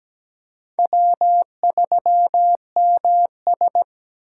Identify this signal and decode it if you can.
Morse code, heard as a keyed tone, about 20 wpm, 700 Hz. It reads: W3MS
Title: Speed Racing - USA 1x2 Call Signs 17wpm